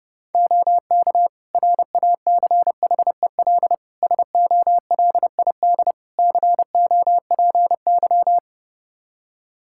Morse code, heard as a keyed tone, about 30 wpm, 700 Hz. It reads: OK RACHEL SOLID COPY
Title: K1USN SST QSO Practice 30wpm